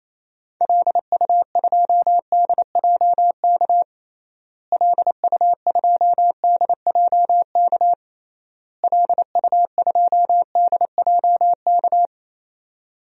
Morse code, heard as a keyed tone, about 28 wpm, 700 Hz. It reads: LU2DJK LU2DJK LU2DJK
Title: DXCC-100 Call Signs - 6 Characters - Repeated 3x - 28wpm